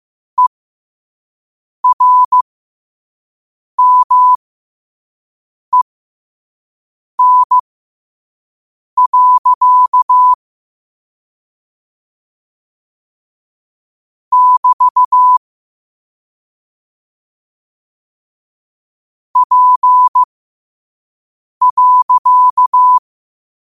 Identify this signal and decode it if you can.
Morse code, heard as a keyed tone, about 15 wpm, spaced as for 5 wpm, 1000 Hz. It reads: ERMEN. <BT> P.
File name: qotdfiles / 5wpm